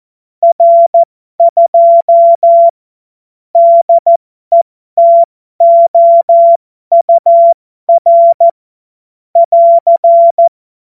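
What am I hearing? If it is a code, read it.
Morse code, heard as a keyed tone, about 14 wpm, spaced as for 12 wpm, 680 Hz. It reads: R2 DETOUR <AR>
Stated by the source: Morse joke practice at effective 60, Farnsworth 70.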